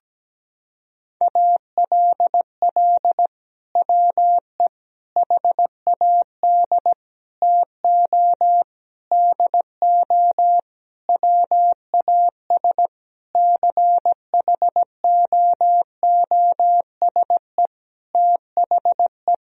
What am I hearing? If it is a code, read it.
Morse code, heard as a keyed tone, about 17 wpm, 700 Hz. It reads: ALL WE HAD TO DO WAS CHOOSE THE
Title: Sentences from Top 1000 Words - Review Part 4 - 17wpm